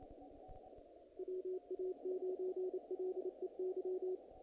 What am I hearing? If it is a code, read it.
Morse code, heard as a keyed tone, about 28 wpm, 370 Hz. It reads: WA9LEY
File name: contests / naqp-cw-k5tr-op-k5pi- -1900